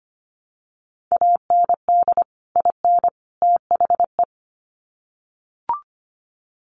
Morse code, heard as a keyed tone, about 25 wpm, 700 Hz. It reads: AND IN THE
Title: Sets of 3 Words 25wpm